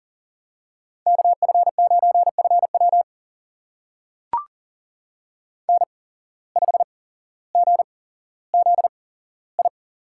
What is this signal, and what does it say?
Morse code, heard as a keyed tone, about 39 wpm, spaced as for 17 wpm, 700 Hz. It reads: KF9FW N5GZI
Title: USA Call Signs - 5 Characters - 8x Character Spacing - 40wpm